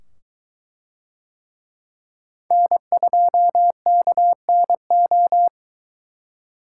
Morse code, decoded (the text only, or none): N2KNO